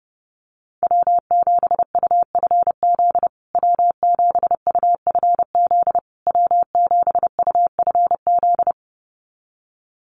Morse code, heard as a keyed tone, about 30 wpm, 700 Hz. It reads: W7UFZ W7UFZ W7UFZ
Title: USA Call Signs Repeated 3x 30wpm